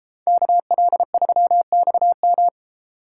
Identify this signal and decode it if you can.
Morse code, heard as a keyed tone, about 32 wpm, 700 Hz. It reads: KL3XM